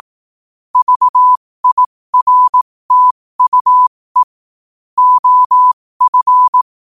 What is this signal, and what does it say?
Morse code, heard as a keyed tone, about 18 wpm, spaced as for 15 wpm, 1000 Hz. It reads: VIRTUE OF
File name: qotdfiles / 15wpm